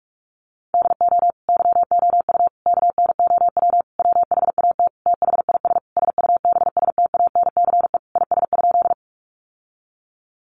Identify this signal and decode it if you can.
Morse code, heard as a keyed tone, about 44 wpm, 700 Hz. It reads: DO YOU KNOW WHAT THIS SUBSTANCE IS?
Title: Sentences from Top 1500 Words 45wpm